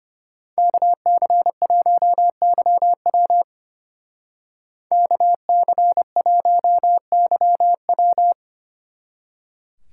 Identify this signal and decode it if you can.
Morse code, heard as a keyed tone, about 30 wpm, 700 Hz. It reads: KC1YW KC1YW
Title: Speed Racing - USA Call Signs 25wpm